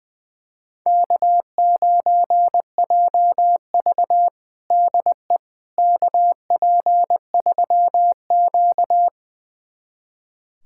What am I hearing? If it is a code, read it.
Morse code, heard as a keyed tone, about 21 wpm, 700 Hz. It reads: K9JV DE KP3Q
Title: USA Callsign DE Callsign - 4 Characters - 20wpm